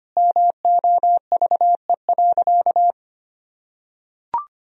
Morse code, heard as a keyed tone, about 25 wpm, 700 Hz. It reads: MOVE.
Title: Sentences from Top 1000 Words - Review Part 1 - 25wpm